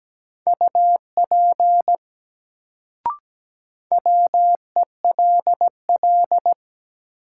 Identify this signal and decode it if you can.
Morse code, heard as a keyed tone, about 17 wpm, 700 Hz. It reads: UP WELL
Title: Sets of 2 Words - New Word Drill - 17wpm